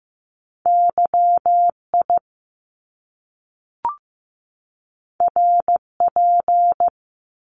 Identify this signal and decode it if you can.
Morse code, heard as a keyed tone, about 15 wpm, 700 Hz. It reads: YI RP
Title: Top Two Letter Combinations in Words 15wpm